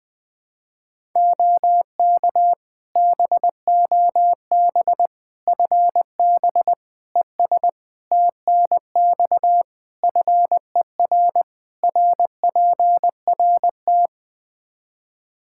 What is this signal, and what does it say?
Morse code, heard as a keyed tone, about 20 wpm, 700 Hz. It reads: OK BOB FB ES TNX FER RPRT
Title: Practice QSO - Exchange 2 - Info - 20wpm